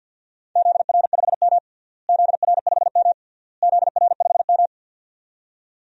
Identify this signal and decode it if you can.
Morse code, heard as a keyed tone, about 52 wpm, 700 Hz. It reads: ZR5M ZR5M ZR5M
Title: DXCC-100 Call Signs - 4 Characters - Repeated 3x - 50wpm